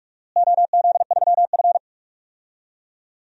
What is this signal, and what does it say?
Morse code, heard as a keyed tone, about 46 wpm, 700 Hz. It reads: OZ3F
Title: DXCC-100 Call Signs - 4 Characters - Repeated 3x - 45wpm